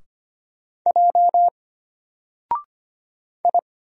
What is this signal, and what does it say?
Morse code, heard as a keyed tone, about 25 wpm, 700 Hz. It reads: J I
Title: Single Letter-Number 25wpm